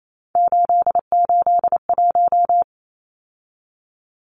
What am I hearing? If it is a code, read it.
Morse code, heard as a keyed tone, about 28 wpm, 700 Hz. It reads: 881
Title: Numbers - 3-Digit 28wpm